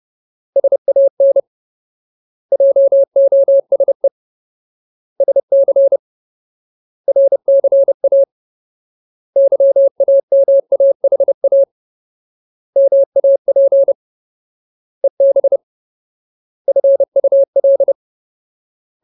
Morse code, heard as a keyed tone, about 30 wpm, 550 Hz. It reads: SAN JOSE SC RCA YAMAHA MAP EB FUL
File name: Ultimate Word List / 30WPM Ultimate